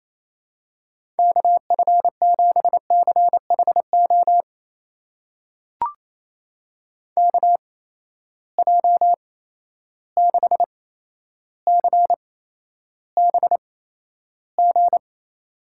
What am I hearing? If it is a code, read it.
Morse code, heard as a keyed tone, about 28 wpm, spaced as for 12 wpm, 700 Hz. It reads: KF7CHO KJ6CBG
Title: USA Call Signs - 6 Characters - 8x Character Spacing - 28wpm